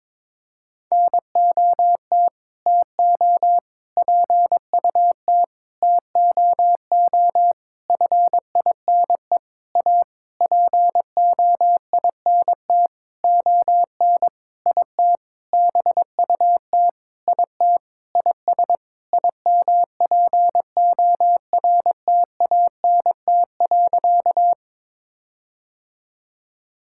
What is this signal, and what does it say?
Morse code, heard as a keyed tone, about 21 wpm, 700 Hz. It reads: NOT TO PUT TOO FINE A POINT ON IT BUT IT IS IMPORTANT.
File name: practice / 22wpm / Sentences with 8-Word Phrases - 22wpm